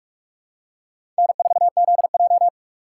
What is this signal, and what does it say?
Morse code, heard as a keyed tone, about 46 wpm, 690 Hz. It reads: N4ZJ
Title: USA Call Signs - 4 Characters - 3x Character Spacing - 45wpm